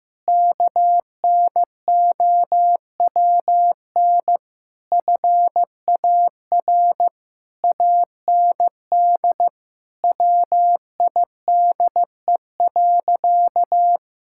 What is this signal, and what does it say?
Morse code, heard as a keyed tone, about 15 wpm, 700 Hz. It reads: KNOWN FAR AND WIDE.
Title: Sentences from Top 1000 Words - Review Part 6 - 15wpm